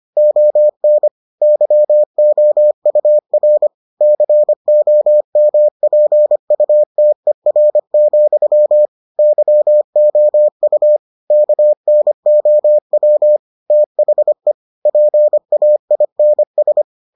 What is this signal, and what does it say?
Morse code, heard as a keyed tone, about 25 wpm, 600 Hz. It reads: ON YOUR COMPUTER, YOU KNOW THE PAINS